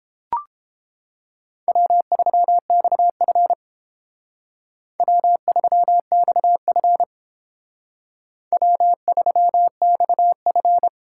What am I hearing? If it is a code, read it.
Morse code, heard as a keyed tone, about 31 wpm, 700 Hz. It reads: W3XF W3XF W3XF